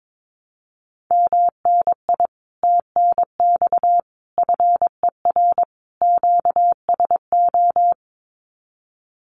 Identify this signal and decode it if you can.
Morse code, heard as a keyed tone, about 22 wpm, 700 Hz. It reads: MNI TNX FER QSO
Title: Common QSO Phrases Encore 22wpm